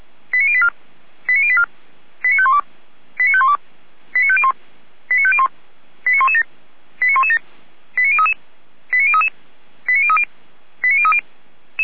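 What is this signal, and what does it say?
Five-tone selective call (ZVEI), heard as quick runs of five tones, tones of about 70 ms. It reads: 89084 89084 89512 89512 89501 89501 89117 89117 89033 89033 89030 89030
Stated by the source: ZVEI five-tone selective call 89057 received by SDR.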